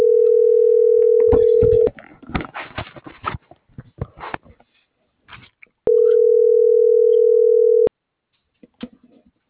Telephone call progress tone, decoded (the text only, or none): ringback tone